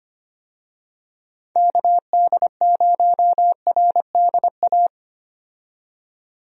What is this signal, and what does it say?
Morse code, heard as a keyed tone, about 25 wpm, 700 Hz. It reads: KD0RDA
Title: USA Call Signs - 6 Characters - 25wpm